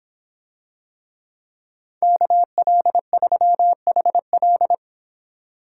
Morse code, heard as a keyed tone, about 24 wpm, 700 Hz. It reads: KL3HL